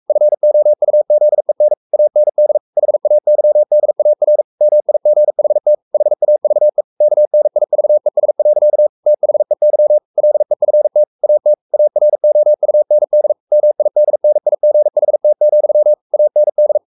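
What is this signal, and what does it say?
Morse code, heard as a keyed tone, about 44 wpm, 600 Hz. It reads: FOUZEN AND HAYDAR MIGHT HAVE KNIVES. THEY LEFT AT AROUND MIDNIGHT, AND
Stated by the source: Morse code at 43 wpm sending Ken Follett's novel Never in English, from CW med Gnister.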